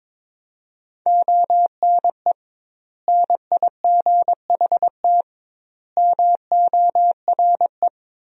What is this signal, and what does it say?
Morse code, heard as a keyed tone, about 22 wpm, 700 Hz. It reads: ONE NIGHT MORE
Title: 2x Word-Spacing - Sentences from Top 300 Words - 22wpm